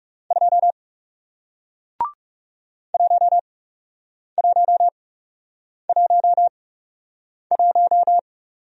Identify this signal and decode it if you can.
Morse code, heard as a keyed tone, about 40 wpm, spaced as for 8 wpm, 700 Hz. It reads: 2 1111